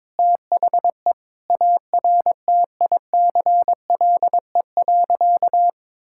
Morse code, heard as a keyed tone, about 22 wpm, 700 Hz. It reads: THE ARTICLE.
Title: Sentences from Top 1000 Words - Review Part 5 - 22wpm